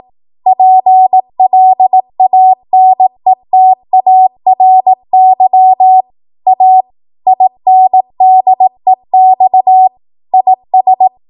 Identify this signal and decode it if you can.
Morse code, heard as a keyed tone, about 18 wpm, 750 Hz. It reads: PLANETARY A INDEX IS
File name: Archive / 18 WPM / 18WPM